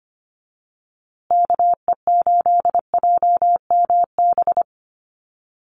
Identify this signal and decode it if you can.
Morse code, heard as a keyed tone, about 25 wpm, 700 Hz. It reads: KE8JMB
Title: USA Call Signs Repeated 3x 25wpm